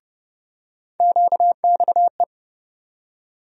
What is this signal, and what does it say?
Morse code, heard as a keyed tone, about 30 wpm, 700 Hz. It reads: QXE
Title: USA Call Sign Postfixes - 3 Letters 30wpm